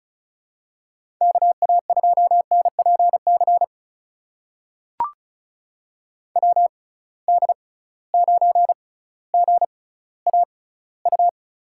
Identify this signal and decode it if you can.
Morse code, heard as a keyed tone, about 36 wpm, spaced as for 18 wpm, 700 Hz. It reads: KA2NPC WD9GAU
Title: USA Call Signs - 6 Characters - 6x Character Spacing - 35wpm